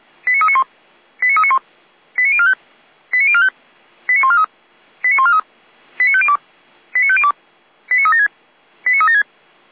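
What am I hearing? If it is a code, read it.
Five-tone selective call (ZVEI), heard as quick runs of five tones, tones of about 70 ms. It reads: 89391 89391 89045 89045 89143 89143 89502 89502 89376 89376
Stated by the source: ZVEI five-tone call received on an SDR.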